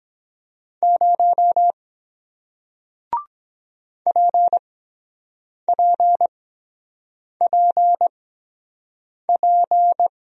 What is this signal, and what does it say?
Morse code, heard as a keyed tone, about 23 wpm, spaced as for 7 wpm, 700 Hz. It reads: 0 PPPP